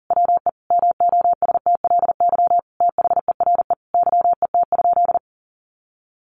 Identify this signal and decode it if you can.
Morse code, heard as a keyed tone, about 40 wpm, 700 Hz. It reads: WE MOSTLY THERE YET?
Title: Sentences from Top 1300 Words 40wpm